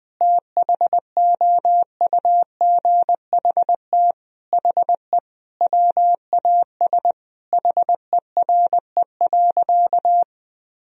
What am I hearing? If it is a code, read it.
Morse code, heard as a keyed tone, about 20 wpm, 700 Hz. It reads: THOUGHT HE WAS HERE.
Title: Sentences from Top 1000 Words - Review Part 3 - 20wpm